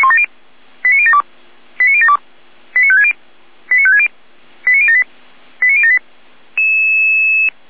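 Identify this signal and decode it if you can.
Five-tone selective call (ZVEI), heard as quick runs of five tones, tones of about 70 ms. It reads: 89072 89072 89560 89560 89078 89078
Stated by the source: SDR-received ZVEI five-tone selective call.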